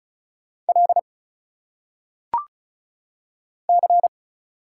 Morse code, heard as a keyed tone, about 36 wpm, 700 Hz. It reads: L C
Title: Single Letters - Phonetic Pronunciation 35wpm